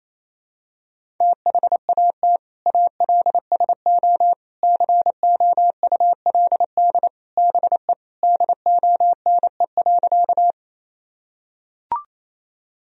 Morse code, heard as a keyed tone, about 27 wpm, 700 Hz. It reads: THAT ALSO COULD BE DONE.
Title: Sentences from Top 1000 Words - Review Part 1 - 28wpm